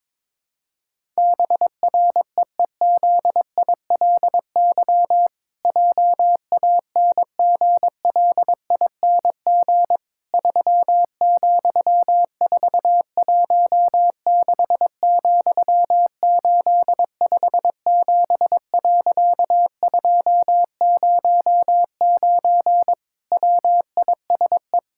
Morse code, heard as a keyed tone, about 22 wpm, 700 Hz. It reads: BREEZILY JANGLING 3,416,857.209 WISE
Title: Pangram Challenge v2 - 22wpm